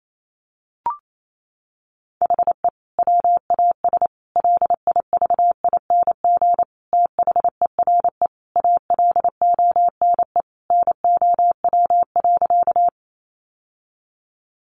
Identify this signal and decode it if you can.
Morse code, heard as a keyed tone, about 28 wpm, 700 Hz. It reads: HE WAS LIVING THERE ALONE NOW.
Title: Sentences from Top 1100 Words 28wpm